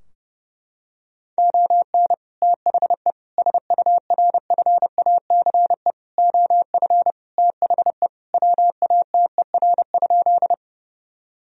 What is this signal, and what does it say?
Morse code, heard as a keyed tone, about 30 wpm, 700 Hz. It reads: ON THE SURFACE OF THE WATER?
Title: Sentences from Top 1000 Words - Review Part 3 - 30wpm